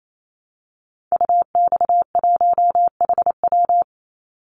Morse code, heard as a keyed tone, about 28 wpm, 700 Hz. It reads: UX1HW
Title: Call Signs Repeated 2x - 28wpm